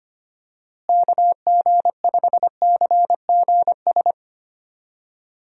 Morse code, heard as a keyed tone, about 25 wpm, 700 Hz. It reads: KG5CGS
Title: USA Call Signs - 6 Characters - Repeated 3x - 25wpm